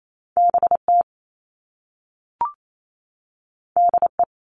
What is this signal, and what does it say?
Morse code, heard as a keyed tone, about 28 wpm, 700 Hz. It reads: BT DE
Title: All Cut Numbers - 2-Digit 28wpm